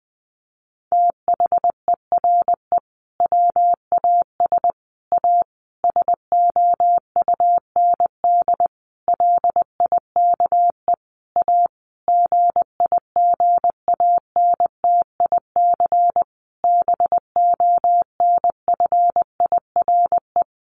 Morse code, heard as a keyed tone, about 20 wpm, 700 Hz. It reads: THERE WAS A SOUND LIKE A GIGANTIC BONFIRE
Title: A Sound Of Thunder 20wpm